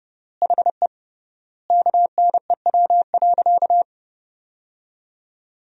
Morse code, heard as a keyed tone, about 32 wpm, 700 Hz. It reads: HE KNEW.
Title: 3x Word-Spacing - Sentences from Top 400 Words - 30wpm